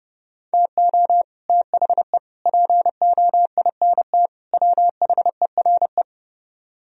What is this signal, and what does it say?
Morse code, heard as a keyed tone, about 31 wpm, 700 Hz. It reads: TO THE POINT WHERE